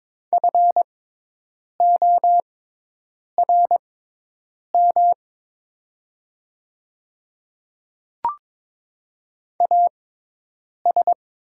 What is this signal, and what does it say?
Morse code, heard as a keyed tone, about 22 wpm, spaced as for 8 wpm, 700 Hz. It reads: FORM AS
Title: 6x Character Spacing - Top 200 Words - 22wpm